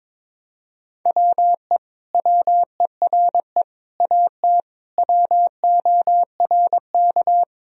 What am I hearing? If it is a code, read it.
Morse code, heard as a keyed tone, about 22 wpm, 700 Hz. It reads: WE WERE AT WORK
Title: Sentences from Top 1000 Words - Review Part 1 - 22wpm